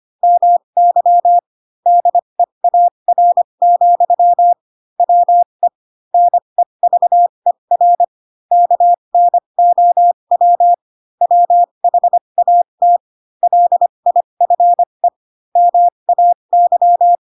Morse code, heard as a keyed tone, about 25 wpm, spaced as for 22 wpm, 700 Hz. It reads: MY DEAR, WE NEVER KNOW WHAT LIFE MAY
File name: Pinocchio / Pinocchio ch7 18 25 ik1hge